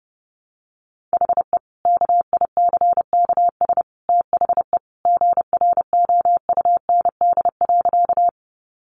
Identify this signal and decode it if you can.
Morse code, heard as a keyed tone, about 30 wpm, 700 Hz. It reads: HE KICKS THE GROUND.
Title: Additional Easy Phrases 30wpm